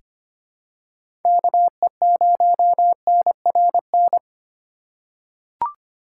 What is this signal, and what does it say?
Morse code, heard as a keyed tone, about 25 wpm, 700 Hz. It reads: KE0NRN
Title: USA Call Signs - 2x Character Spacing - 25wpm